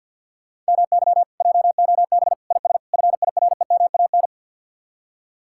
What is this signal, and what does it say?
Morse code, heard as a keyed tone, about 62 wpm, 700 Hz. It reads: MY JOB IS FIREMAN
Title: Speed-Racing Common QSO Phrases - 50wpm